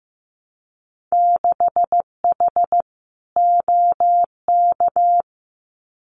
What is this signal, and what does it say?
Morse code, heard as a keyed tone, about 15 wpm, 700 Hz. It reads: 6H OK
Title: Winter Field Day Exchanges Encore 15wpm